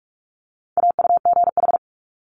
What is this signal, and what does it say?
Morse code, heard as a keyed tone, about 46 wpm, 700 Hz. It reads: AUGH
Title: Multi-letter Phonemes Quick Fire 45wpm